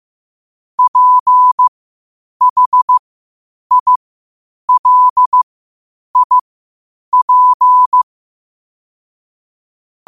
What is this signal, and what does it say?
Morse code, heard as a keyed tone, about 15 wpm, spaced as for 8 wpm, 1000 Hz. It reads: PHILIP
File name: qotdfiles / 8wpm